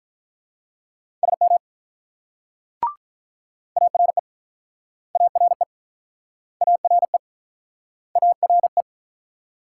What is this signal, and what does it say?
Morse code, heard as a keyed tone, about 48 wpm, 700 Hz. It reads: SM ARE ARE ARE ARE